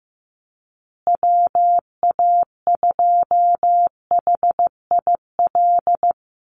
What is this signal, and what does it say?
Morse code, heard as a keyed tone, about 15 wpm, 700 Hz. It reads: WA2HIL
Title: USA Call Signs Repeated 3x 15wpm